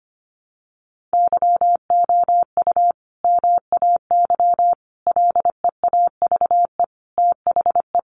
Morse code, heard as a keyed tone, about 25 wpm, 700 Hz. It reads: YOU MAY LEAVE THE
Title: Sentences from Top 400 Words 25wpm